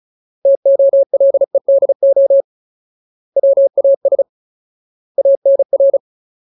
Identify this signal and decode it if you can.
Morse code, heard as a keyed tone, about 34 wpm, 550 Hz. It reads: TOLEDO WAS ANR
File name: Ultimate Word List / 35WPM Ultimate